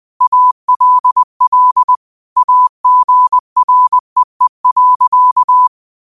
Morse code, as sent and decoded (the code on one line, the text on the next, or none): .- .-.. .-.. / .- --. .-. . .-.-.-
ALL AGREE.